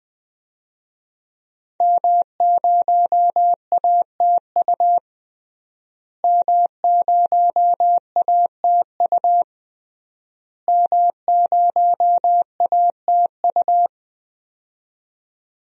Morse code, heard as a keyed tone, about 20 wpm, 700 Hz. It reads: M0ATU M0ATU M0ATU
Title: DXCC-100 Call Signs - 5 Characters - Repeated 3x - 20wpm